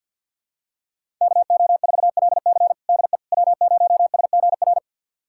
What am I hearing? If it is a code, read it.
Morse code, heard as a keyed tone, about 52 wpm, 700 Hz. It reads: KO4LC DE W0SGR
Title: USA Callsign DE Callsign - 5 Characters - 50wpm